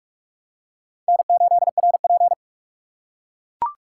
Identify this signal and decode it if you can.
Morse code, heard as a keyed tone, about 39 wpm, 700 Hz. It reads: N8RP